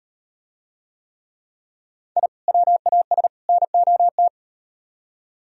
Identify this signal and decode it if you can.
Morse code, heard as a keyed tone, about 36 wpm, 700 Hz. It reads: I WAS NOT